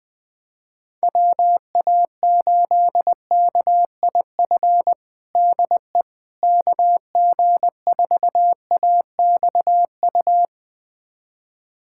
Morse code, heard as a keyed tone, about 20 wpm, 700 Hz. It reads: WA8KIF DE KG4AXU
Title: USA Callsign DE Callsign - 6 Characters - 20wpm